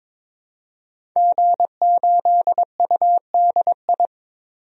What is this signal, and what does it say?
Morse code, heard as a keyed tone, about 22 wpm, 700 Hz. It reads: G8UDI
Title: DXCC-100 Call Signs - 5 Characters - 22wpm